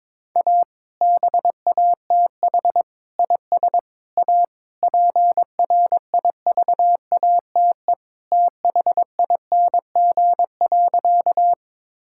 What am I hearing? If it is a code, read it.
Morse code, heard as a keyed tone, about 23 wpm, 700 Hz. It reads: A BATH IS A PRIVATE THING.
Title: Sentences from Top 1600 Words 22wpm